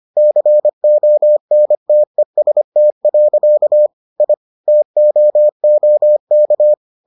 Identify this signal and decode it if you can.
Morse code, heard as a keyed tone, about 25 wpm, 600 Hz. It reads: CONTEST. I TOOK